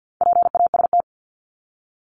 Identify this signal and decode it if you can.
Morse code, heard as a keyed tone, about 50 wpm, 700 Hz. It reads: PAST
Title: Top 501-750 Words 50wpm